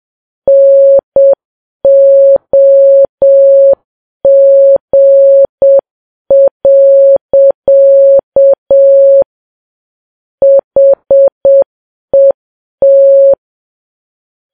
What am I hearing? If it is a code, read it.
Morse code, heard as a keyed tone, about 7 wpm, 550 Hz. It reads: NOG. HET
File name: Files Web tls / Morse on9fv / morse7kps